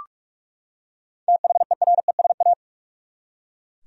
Morse code, heard as a keyed tone, about 48 wpm, 700 Hz. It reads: THERESA
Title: USA Common Names - 45wpm